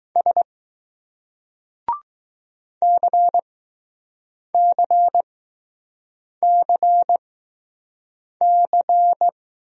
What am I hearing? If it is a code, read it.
Morse code, heard as a keyed tone, about 20 wpm, spaced as for 6 wpm, 700 Hz. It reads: S CCCC